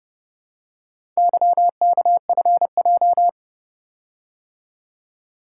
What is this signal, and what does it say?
Morse code, heard as a keyed tone, about 29 wpm, 700 Hz. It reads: YKFJ